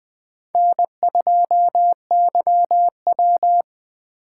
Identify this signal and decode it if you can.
Morse code, heard as a keyed tone, about 20 wpm, 700 Hz. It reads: N2YW
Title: USA Call Signs - 4 Characters - 2x Character Spacing - 20wpm